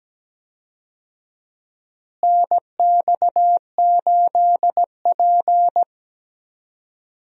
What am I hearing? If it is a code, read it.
Morse code, heard as a keyed tone, about 17 wpm, 700 Hz. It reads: NX8P
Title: USA Call Signs - 4 Characters - Repeated 2x - 17wpm